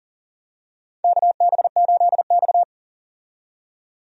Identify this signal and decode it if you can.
Morse code, heard as a keyed tone, about 40 wpm, 700 Hz. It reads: KB8X
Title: USA Call Signs - 4 Characters - 40wpm